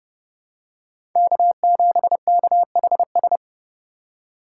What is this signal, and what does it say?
Morse code, heard as a keyed tone, about 31 wpm, 700 Hz. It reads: K7KHS
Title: USA Call Signs - 2x Character Spacing - 30wpm